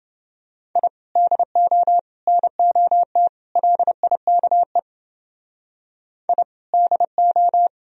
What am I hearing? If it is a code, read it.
Morse code, heard as a keyed tone, about 29 wpm, 700 Hz. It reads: I DO NOT LIKE I DO